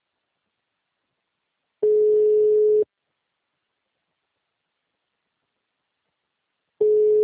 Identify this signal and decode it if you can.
Ringback tone (Europe, 425 Hz).